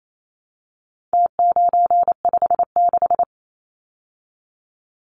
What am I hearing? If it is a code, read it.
Morse code, heard as a keyed tone, about 28 wpm, 700 Hz. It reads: T956
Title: Numbers - Contest Style - 4-Digit 28wpm